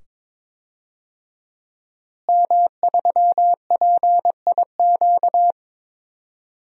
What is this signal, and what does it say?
Morse code, heard as a keyed tone, about 22 wpm, 700 Hz. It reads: M3PIQ
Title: DXCC-100 Call Signs - 5 Characters - Repeated 2x - 22wpm